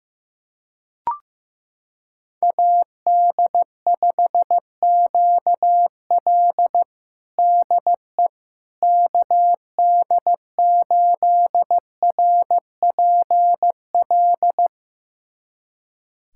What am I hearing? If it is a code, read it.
Morse code, heard as a keyed tone, about 15 wpm, 700 Hz. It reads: AD5QL DE KD8RPL
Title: USA Callsign DE Callsign 15wpm